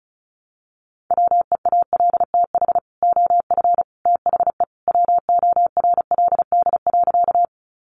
Morse code, heard as a keyed tone, about 34 wpm, 700 Hz. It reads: WEALTH OF THE WORLD.
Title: Sentences from Top 1200 Words 35wpm